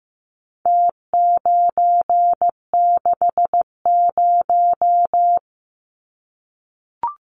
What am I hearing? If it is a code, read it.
Morse code, heard as a keyed tone, about 15 wpm, 700 Hz. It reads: T960
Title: Numbers - Contest Style - 4-Digit 15wpm